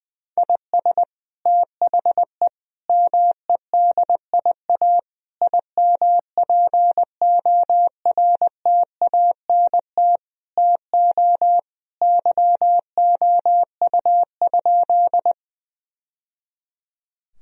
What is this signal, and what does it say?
Morse code, heard as a keyed tone, about 20 wpm, 700 Hz. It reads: IS THE MEDIA IMPORTANT TO YOU?
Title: Sentences from Top 1000 Words - Review Part 4 - 20wpm